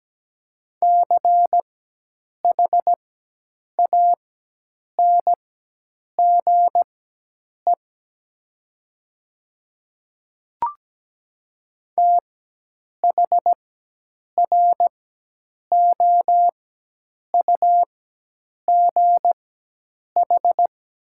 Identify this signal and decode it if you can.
Morse code, heard as a keyed tone, about 17 wpm, spaced as for 8 wpm, 700 Hz. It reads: CHANGE THROUGH
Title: 4x Character Spacing - Top 500 Words - 17wpm